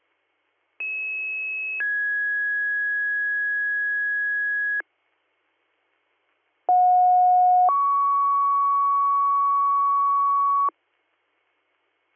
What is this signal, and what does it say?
Two-tone page, heard as a one-second tone, then a three-second tone, about 2600 then 1650 Hz, about 730 then 1120 Hz. It is heard from 0.8 to 10.7 s.